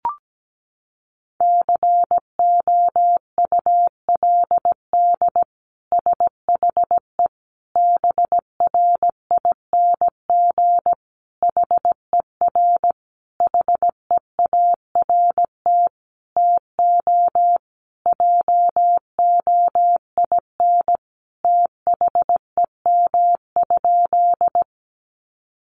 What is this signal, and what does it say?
Morse code, heard as a keyed tone, about 17 wpm, 700 Hz. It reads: COULD SHE BRING HER HEART TO JOIN THEM?
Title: Sentences from Top 1400 Words 17wpm